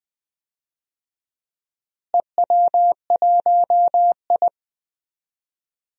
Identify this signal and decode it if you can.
Morse code, heard as a keyed tone, about 20 wpm, 700 Hz. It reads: EW1I
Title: DXCC-100 Call Signs - 4 Characters - Repeated 3x - 20wpm